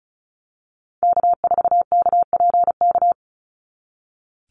Morse code, heard as a keyed tone, about 34 wpm, 700 Hz. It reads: K4KPK
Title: NA SOTA Activator Callsigns 35wpm